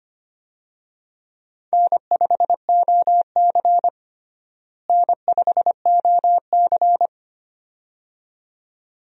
Morse code, heard as a keyed tone, about 25 wpm, 700 Hz. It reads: N5OC N5OC
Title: USA Call Signs - 4 Characters - Repeated 2x - 25wpm